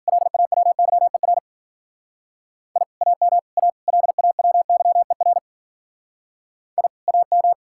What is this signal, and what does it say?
Morse code, heard as a keyed tone, about 49 wpm, 700 Hz. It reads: LAWYER I AM A LAWYER I AM
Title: Speed-Racing Common QSO Phrases - 40wpm